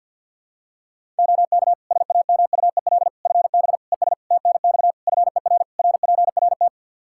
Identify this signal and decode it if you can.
Morse code, heard as a keyed tone, about 53 wpm, 700 Hz. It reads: OK SAMUEL FB ES TNX FER RPRT
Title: Practice QSO - Exchange 2 - Info - 50wpm